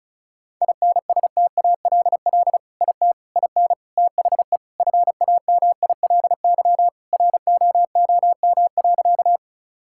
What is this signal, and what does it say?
Morse code, heard as a keyed tone, about 36 wpm, 700 Hz. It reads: INSTALL IT IN THE FAMILY ROOM.
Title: Sentences from Top 1600 Words 35wpm